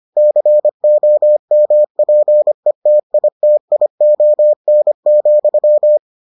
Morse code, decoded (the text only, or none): COMPETITION,